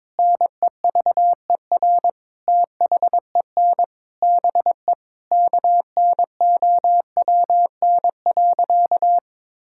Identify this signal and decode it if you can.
Morse code, heard as a keyed tone, about 22 wpm, 700 Hz. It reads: NEVER THEN BE KNOWN.
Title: Sentences from Top 1600 Words 22wpm